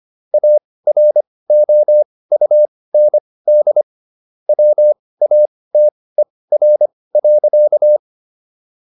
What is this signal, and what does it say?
Morse code, heard as a keyed tone, about 25 wpm, spaced as for 18 wpm, 600 Hz. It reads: AROUND WATER.
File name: SS108 18